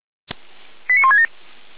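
Five-tone selective call (ZVEI), heard as quick runs of five tones, tones of about 70 ms. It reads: 89167
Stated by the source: ZVEI five-tone call received on SDR, from audio.f85360.feuerwehr-orlamuende.org (fire brigade Orlamünde).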